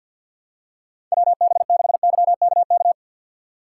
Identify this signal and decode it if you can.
Morse code, heard as a keyed tone, about 53 wpm, 700 Hz. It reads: WB6YKK
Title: USA Call Signs - 6 Characters - 4x Character Spacing - 50wpm